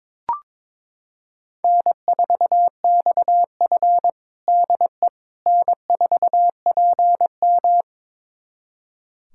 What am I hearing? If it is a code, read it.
Morse code, heard as a keyed tone, about 22 wpm, 700 Hz. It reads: N4XF DE N4PM
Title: USA Callsign DE Callsign - 4 Characters - 22wpm